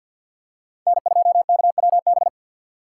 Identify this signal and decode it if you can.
Morse code, heard as a keyed tone, about 53 wpm, 700 Hz. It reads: N2KWB
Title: USA Call Signs - 5 Characters - 3x Character Spacing - 50wpm